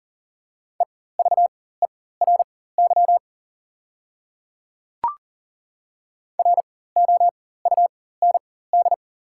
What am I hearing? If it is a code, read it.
Morse code, heard as a keyed tone, about 40 wpm, spaced as for 19 wpm, 690 Hz. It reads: EVERY ROUND
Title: 4x Character Spacing - Top 200 Words - 40wpm